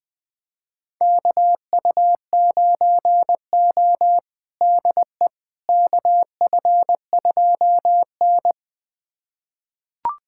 Morse code, heard as a keyed tone, about 21 wpm, 700 Hz. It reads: KU9O DE KF2N
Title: USA Callsign DE Callsign - 4 Characters - 20wpm